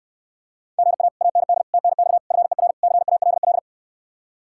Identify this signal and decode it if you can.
Morse code, heard as a keyed tone, about 68 wpm, 700 Hz. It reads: GA AND TNX FER CALL